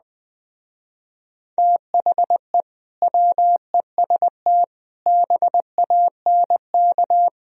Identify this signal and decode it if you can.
Morse code, heard as a keyed tone, about 20 wpm, 700 Hz. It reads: THE WEST BANK
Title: Speed-Racing - Sets of 5 Words 17wpm